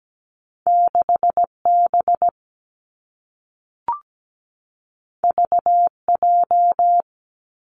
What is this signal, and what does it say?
Morse code, heard as a keyed tone, about 17 wpm, 700 Hz. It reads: 6 B V J
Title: Two Challenging Letter-Number Combinations 17wpm